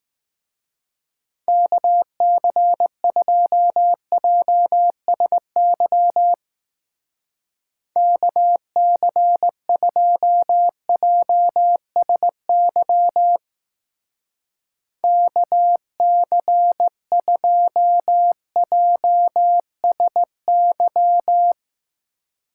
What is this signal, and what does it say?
Morse code, heard as a keyed tone, about 18 wpm, 700 Hz. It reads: KC2JSY KC2JSY KC2JSY
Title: Speed Racing - USA Call Signs 15wpm